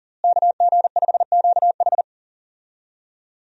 Morse code, heard as a keyed tone, about 40 wpm, 700 Hz. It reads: KG5QH
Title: USA Call Signs - 5 Characters - Repeated 3x - 40wpm